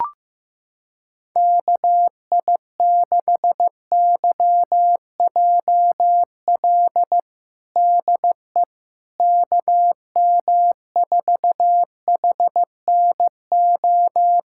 Morse code, heard as a keyed tone, about 15 wpm, 700 Hz. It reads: KI6YJL DE KM4HNO
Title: USA Callsign DE Callsign - 6 Characters - 15wpm